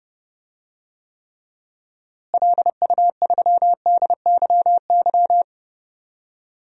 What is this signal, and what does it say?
Morse code, heard as a keyed tone, about 30 wpm, 700 Hz. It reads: LU3DYY
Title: DXCC-100 Call Signs - 6 Characters - Repeated 2x - 30wpm